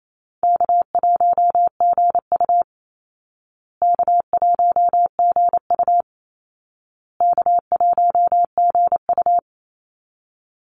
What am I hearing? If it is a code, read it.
Morse code, heard as a keyed tone, about 28 wpm, 700 Hz. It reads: K1GU K1GU K1GU
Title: Easy Call Signs Repeated 3x - 28wpm